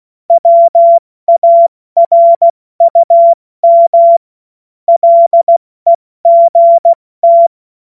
Morse code, heard as a keyed tone, about 16 wpm, spaced as for 14 wpm, 680 Hz. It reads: WARUM LEGT
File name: Morsekurs / Witze / Witz 10.Oktober 2023 70 F80